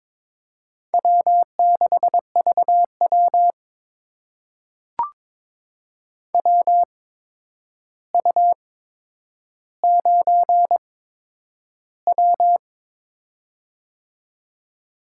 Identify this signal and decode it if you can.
Morse code, heard as a keyed tone, about 22 wpm, spaced as for 9 wpm, 700 Hz. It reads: W6VW WU9W
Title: USA Call Signs - 4 Characters - 8x Character Spacing - 22wpm